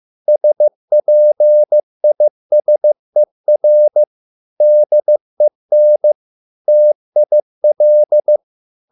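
Morse code, heard as a keyed tone, about 15 wpm, 600 Hz. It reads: SPISER DEN TIL
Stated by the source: Morse code at a slow 15 wpm from CW med Gnister.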